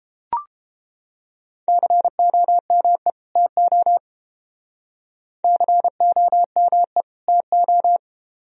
Morse code, heard as a keyed tone, about 30 wpm, 700 Hz. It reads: COME TO COME TO